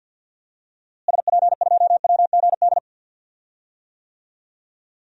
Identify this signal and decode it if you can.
Morse code, heard as a keyed tone, about 50 wpm, 700 Hz. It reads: SP2WGD
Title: DXCC-100 Call Signs - 6 Characters - Repeated 2x - 50wpm